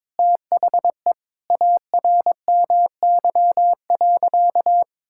Morse code, heard as a keyed tone, about 21 wpm, 700 Hz. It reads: THE ARMY.
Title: Sentences from Top 1600 Words 22wpm